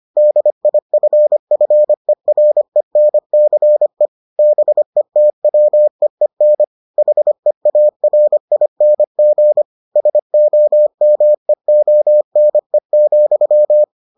Morse code, heard as a keyed tone, about 25 wpm, 600 Hz. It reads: DIFFERENCE BETWEEN HEARING SOMEONE,